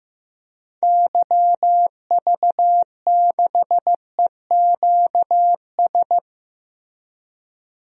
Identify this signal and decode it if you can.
Morse code, heard as a keyed tone, about 16 wpm, 700 Hz. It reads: YV6EQS
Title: DXCC-100 Call Signs - 6 Characters - Repeated 3x - 15wpm